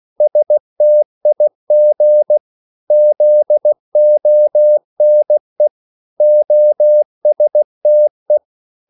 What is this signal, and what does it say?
Morse code, heard as a keyed tone, about 16 wpm, 600 Hz. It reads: STIG ZONE OSTE
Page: Korte danske ord 16wpm